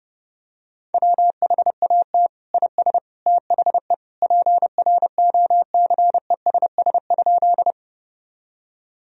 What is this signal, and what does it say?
Morse code, heard as a keyed tone, about 32 wpm, 700 Hz. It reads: WHAT IS THE PROCESS?
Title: Sentences from Top 1000 Words - Review Part 3 - 30wpm